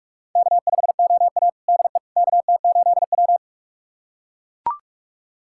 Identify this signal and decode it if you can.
Morse code, heard as a keyed tone, about 43 wpm, 700 Hz. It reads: K5OA DE KT8W
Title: USA Callsign DE Callsign - 4 Characters - 45wpm